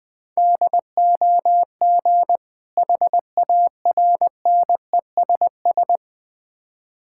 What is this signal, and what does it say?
Morse code, heard as a keyed tone, about 20 wpm, 700 Hz. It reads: DOG HARNESS
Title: Difficult Words with Hints - 20wpm